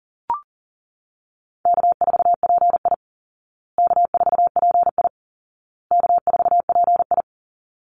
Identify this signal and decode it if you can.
Morse code, heard as a keyed tone, about 37 wpm, 700 Hz. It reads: K4PI K4PI K4PI
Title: Easy Call Signs Repeated 3x - 40wpm